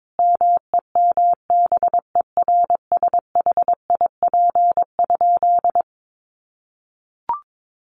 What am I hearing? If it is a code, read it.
Morse code, heard as a keyed tone, about 22 wpm, 700 Hz. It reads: MEMBERSHIP?
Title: Sentences from Top 1400 Words 22wpm